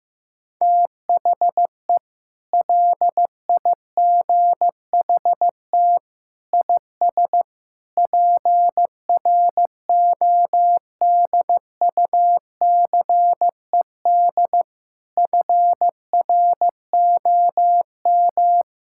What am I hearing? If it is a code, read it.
Morse code, heard as a keyed tone, about 15 wpm, 700 Hz. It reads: THE LIGHT IS PRODUCED FROM
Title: Sentences from Top 1000 Words - Review Part 6 - 15wpm